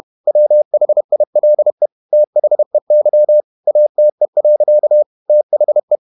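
Morse code, heard as a keyed tone, about 31 wpm, 600 Hz. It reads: WHILE THEY ATE. THE
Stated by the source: Morse code at 30 wpm, the novel Never by Ken Follett in English, from CW med Gnister.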